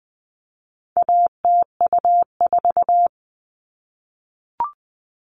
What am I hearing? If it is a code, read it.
Morse code, heard as a keyed tone, about 20 wpm, 700 Hz. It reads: ATU4
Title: All Cut Numbers - 4-Digit 20wpm